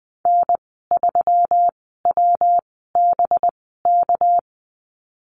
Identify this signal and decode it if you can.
Morse code, heard as a keyed tone, about 20 wpm, spaced as for 15 wpm, 700 Hz. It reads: N3WBK
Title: 200 Lesson 32g - Letter k - Call Signs 20wpm